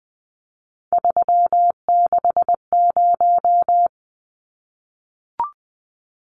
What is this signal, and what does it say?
Morse code, heard as a keyed tone, about 20 wpm, 700 Hz. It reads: 360
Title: Numbers - 3-Digit 20wpm